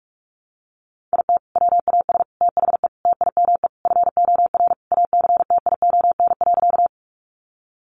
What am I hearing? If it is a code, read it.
Morse code, heard as a keyed tone, about 44 wpm, 700 Hz. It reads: IT WAS THE TIME FOR ACTION.
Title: Sentences from Top 700 Words 45wpm